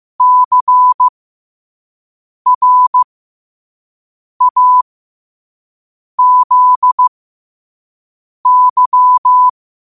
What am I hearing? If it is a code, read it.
Morse code, heard as a keyed tone, about 15 wpm, spaced as for 5 wpm, 1000 Hz. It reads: CRAZY
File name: qotdfiles / 5wpm